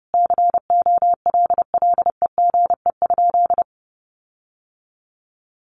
Morse code, heard as a keyed tone, about 30 wpm, 700 Hz. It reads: COLLEGE?
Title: Sentences from Top 1300 Words 30wpm